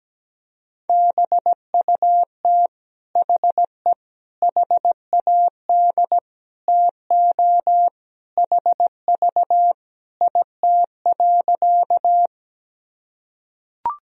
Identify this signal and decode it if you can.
Morse code, heard as a keyed tone, about 17 wpm, 700 Hz. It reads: BUT HE HAD TO HV IT.
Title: Sentences from Top 100 Words and Abbreviations 17wpm